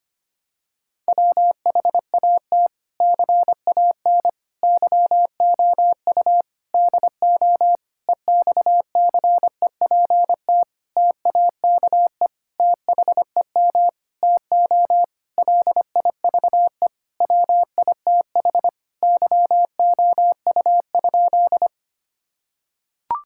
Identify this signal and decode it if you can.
Morse code, heard as a keyed tone, about 25 wpm, 700 Hz. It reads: WHAT CAN YOU DO EXCEPT TAKE THEM TO LIVE WITH YOU?
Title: Sentences from Top 1600 Words 25wpm